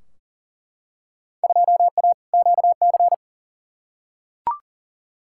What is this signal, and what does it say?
Morse code, heard as a keyed tone, about 39 wpm, 700 Hz. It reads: 2A QC
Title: Field Day Exchanges - 40wpm